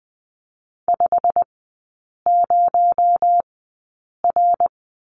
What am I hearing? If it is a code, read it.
Morse code, heard as a keyed tone, about 20 wpm, spaced as for 8 wpm, 700 Hz. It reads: 50R
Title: 218 Lesson 35e - Number 0 - Three Characters 20wpm